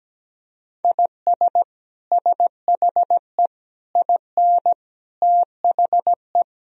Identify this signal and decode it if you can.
Morse code, heard as a keyed tone, about 17 wpm, 700 Hz. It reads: IS SHE IN THE
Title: Sentences from Top 1000 Words - Review Part 2 - 17wpm